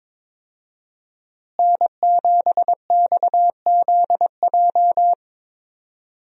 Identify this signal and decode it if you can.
Morse code, heard as a keyed tone, about 22 wpm, 700 Hz. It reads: N7XZJ
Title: USA Call Signs - 5 Characters - Repeated 3x - 22wpm